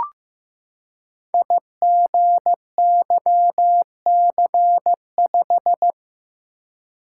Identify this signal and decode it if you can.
Morse code, heard as a keyed tone, about 15 wpm, 700 Hz. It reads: IGYC5
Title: Code Groups - 5 Letter:Numbers - 15wpm